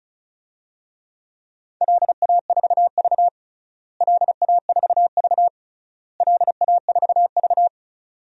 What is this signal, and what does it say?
Morse code, heard as a keyed tone, about 36 wpm, 700 Hz. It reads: LA4V LA4V LA4V
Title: DXCC-100 Call Signs - 4 Characters - Repeated 3x - 35wpm